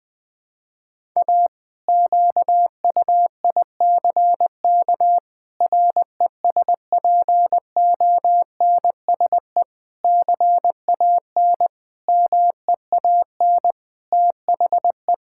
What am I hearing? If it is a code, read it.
Morse code, heard as a keyed tone, about 20 wpm, 700 Hz. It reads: A QUICK RESPONSE CAN MEAN THE
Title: Sentences with 8-Word Phrases - 20wpm